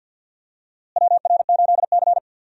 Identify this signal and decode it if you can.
Morse code, heard as a keyed tone, about 53 wpm, 690 Hz. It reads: WR8C
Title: USA Call Signs - 4 Characters - Repeated 2x - 50wpm